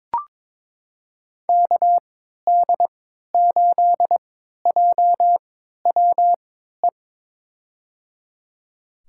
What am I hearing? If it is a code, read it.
Morse code, heard as a keyed tone, about 22 wpm, spaced as for 13 wpm, 700 Hz. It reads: KD8JWE
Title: USA Call Signs - 6 Characters - 3x Character Spacing - 22wpm